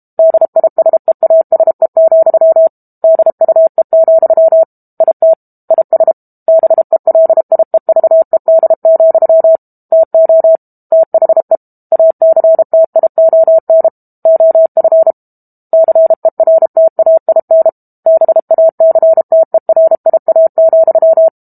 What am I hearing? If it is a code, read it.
Morse code, heard as a keyed tone, about 32 wpm, 650 Hz. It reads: DISEASE, DUE, IT IS BELIEVED, TO THE ACTION OF CERTAIN BACTERIA,
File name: Book Two / 06-Chapter 6 - 32 WPM